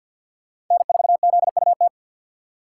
Morse code, heard as a keyed tone, about 46 wpm, 690 Hz. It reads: N4ZUT
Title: USA Call Signs - 5 Characters - 3x Character Spacing - 50wpm